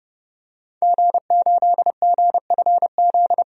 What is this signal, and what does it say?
Morse code, heard as a keyed tone, about 30 wpm, 700 Hz. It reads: G8GFZ
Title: DXCC-100 Call Signs - 5 Characters - 30wpm